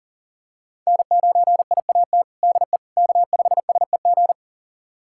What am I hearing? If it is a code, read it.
Morse code, heard as a keyed tone, about 40 wpm, 700 Hz. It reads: N9IAT DE K5SEG
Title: USA Callsign DE Callsign - 5 Characters - 40wpm